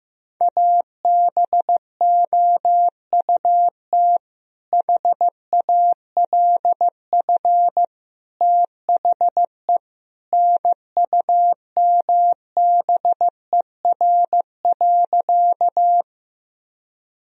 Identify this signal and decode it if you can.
Morse code, heard as a keyed tone, about 16 wpm, 700 Hz. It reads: ABOUT HALF THE NUMBER.
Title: Sentences from Top 1000 Words - Review Part 1 - 15wpm